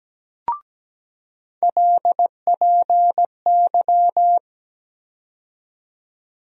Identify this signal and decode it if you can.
Morse code, heard as a keyed tone, about 17 wpm, 700 Hz. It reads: LPY
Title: USA Call Sign Postfixes - 3 Letters 17wpm